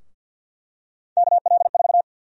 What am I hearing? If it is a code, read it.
Morse code, heard as a keyed tone, about 46 wpm, 700 Hz. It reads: KL4
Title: USA Call Sign Prefixes - 50wpm